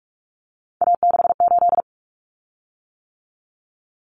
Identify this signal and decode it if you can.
Morse code, heard as a keyed tone, about 44 wpm, 700 Hz. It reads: A68
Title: Common Cut Numbers - 3-Digit 45wpm